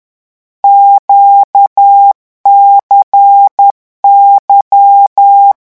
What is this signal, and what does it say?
Morse code, heard as a keyed tone, about 10 wpm, 800 Hz. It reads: QCY
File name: morse mp3 / lecon2-vitesse 10